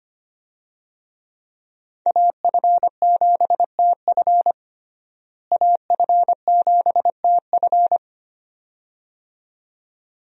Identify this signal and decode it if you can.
Morse code, heard as a keyed tone, about 25 wpm, 700 Hz. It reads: AF7TF AF7TF
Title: USA Call Signs - 5 Characters - Repeated 2x - 25wpm